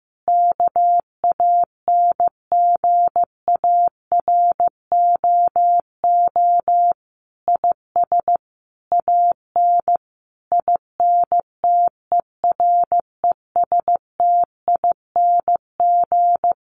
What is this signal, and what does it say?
Morse code, heard as a keyed tone, about 15 wpm, 700 Hz. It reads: KANGAROO IS AN INTERESTING
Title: Easy Phrases 15wpm